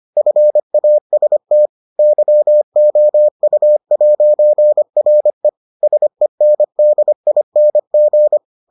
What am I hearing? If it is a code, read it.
Morse code, heard as a keyed tone, about 25 wpm, 600 Hz. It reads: FAST YOU'RE SENDING